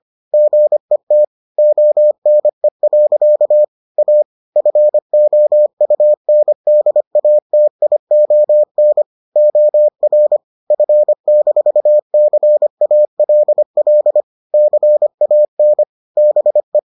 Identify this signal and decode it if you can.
Morse code, heard as a keyed tone, about 25 wpm, 600 Hz. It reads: GET ONE. A FOUNDATION OR F-CALL CAN BE